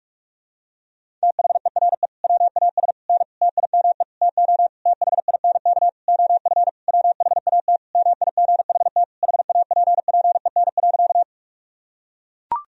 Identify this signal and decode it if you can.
Morse code, heard as a keyed tone, about 41 wpm, 700 Hz. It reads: THERE WAS N TIME TO THINK OF WHAT MIGHT HAPPEN.
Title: Sentences from Top 400 Words and Abbreviations 45wpm